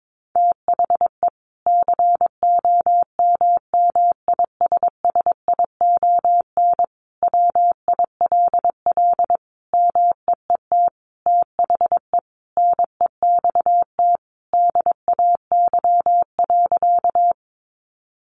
Morse code, heard as a keyed tone, about 22 wpm, 700 Hz. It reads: THE COMMISSION WILL MEET THE NEXT DAY.
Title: Sentences from Top 1400 Words 22wpm